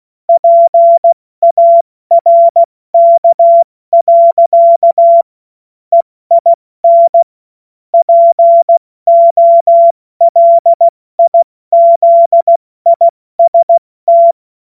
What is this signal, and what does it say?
Morse code, heard as a keyed tone, about 16 wpm, 680 Hz. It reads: PARK. EIN POLIZIST
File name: Morsekurs / Witze / Witz 18. Januar 2022, 70 F80